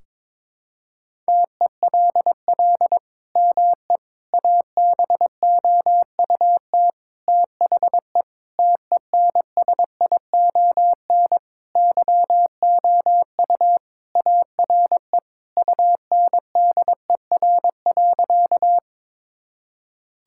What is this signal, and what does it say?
Morse code, heard as a keyed tone, about 22 wpm, 700 Hz. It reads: TELL ME ABOUT THE TENSION YOU ARE UNDER.
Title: Sentences from Top 1600 Words 22wpm